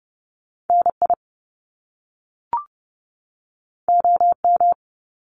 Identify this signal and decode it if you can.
Morse code, heard as a keyed tone, about 30 wpm, 700 Hz. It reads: NI OM
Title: Top Two Letter Combinations in Words 30wpm